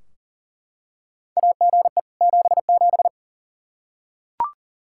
Morse code, heard as a keyed tone, about 40 wpm, 700 Hz. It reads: AGE 77
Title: Practice QSO - Exchange 3 - More Info - 40wpm